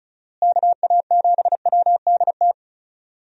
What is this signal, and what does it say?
Morse code, heard as a keyed tone, about 36 wpm, 700 Hz. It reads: KA7WDT
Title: USA Call Signs - 6 Characters - Repeated 3x - 35wpm